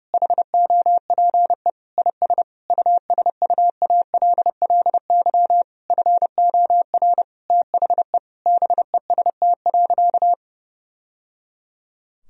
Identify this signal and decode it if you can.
Morse code, heard as a keyed tone, about 30 wpm, 700 Hz. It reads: HOPE IS USUALLY FOR THE BEST.
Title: Sentences from Top 1000 Words - Review Part 3 - 30wpm